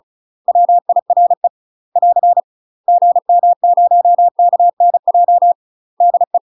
Morse code, heard as a keyed tone, about 34 wpm, 700 Hz. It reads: WIRE <AR> GM0KNJ DE
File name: UK MASTER QSOs TRAINING / 35WPM / UK MASTER QSOs 0014